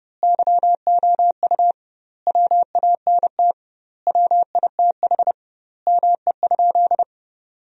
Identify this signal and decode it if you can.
Morse code, heard as a keyed tone, about 30 wpm, 700 Hz. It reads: YOU WANT WITH ME?
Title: 2x Word-Spacing - Sentences from Top 200 Words - 30wpm